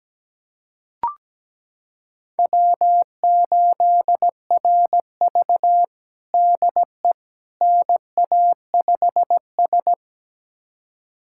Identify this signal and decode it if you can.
Morse code, heard as a keyed tone, about 17 wpm, 700 Hz. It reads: W8RV DE NA5S
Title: USA Callsign DE Callsign - 4 Characters - 17wpm